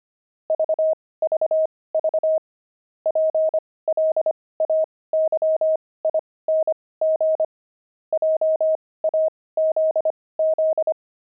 Morse code, heard as a keyed tone, about 25 wpm, spaced as for 18 wpm, 630 Hz. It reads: VVV PLAYING JAZZ
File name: pangram-fs-18-cs-25-jazz